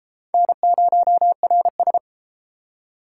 Morse code, heard as a keyed tone, about 28 wpm, 700 Hz. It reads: N0RS